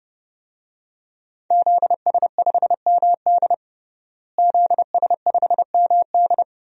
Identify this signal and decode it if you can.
Morse code, heard as a keyed tone, about 32 wpm, 700 Hz. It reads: ZS5MD ZS5MD
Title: DXCC-100 Call Signs - 5 Characters - Repeated 2x - 30wpm